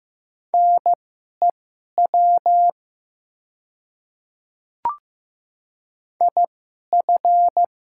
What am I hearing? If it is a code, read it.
Morse code, heard as a keyed tone, about 15 wpm, spaced as for 11 wpm, 700 Hz. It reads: NEW IF
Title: 2x Character Spacing - Top 500 Words - 15wpm